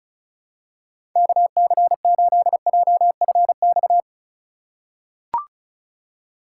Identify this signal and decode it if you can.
Morse code, heard as a keyed tone, about 36 wpm, 700 Hz. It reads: KC8JFX
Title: USA Call Signs - 4x Character Spacing - 35wpm